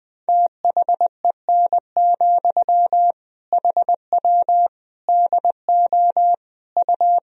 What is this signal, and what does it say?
Morse code, heard as a keyed tone, about 20 wpm, 700 Hz. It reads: THEN, HW DO U
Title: Sentences from Top 100 Words and Abbreviations 20wpm